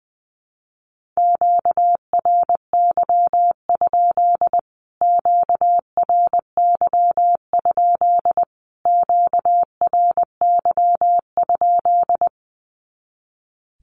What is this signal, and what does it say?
Morse code, heard as a keyed tone, about 20 wpm, 700 Hz. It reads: QRY? QRY? QRY?
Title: Q-Codes Repeated 3x 20wpm